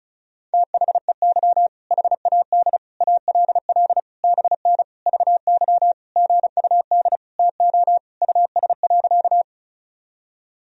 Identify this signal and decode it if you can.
Morse code, heard as a keyed tone, about 36 wpm, 700 Hz. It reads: THEY HAD ALL BN VY GUD TO US.
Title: Sentences from Top 200 Words and Abbreviations 35wpm